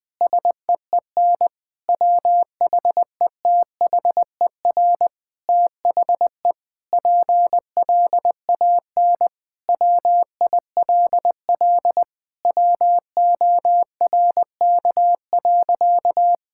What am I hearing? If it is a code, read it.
Morse code, heard as a keyed tone, about 20 wpm, 700 Hz. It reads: SEEN WHETHER THE PLAN WILL WORK.
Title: Sentences with 7-Word Phrases - Spoken Beginning - 20wpm